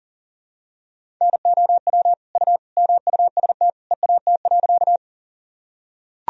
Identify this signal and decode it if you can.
Morse code, heard as a keyed tone, about 40 wpm, 700 Hz. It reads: NOW U MUST EAT.
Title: Sentences from Top 300 Words and Abbreviations 40wpm